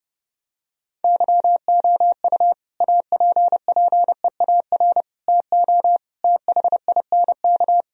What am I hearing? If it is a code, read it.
Morse code, heard as a keyed tone, about 30 wpm, 700 Hz. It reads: YOU APPEAR TO THINK
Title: Sentences from Top 1000 Words - Review Part 2 - 30wpm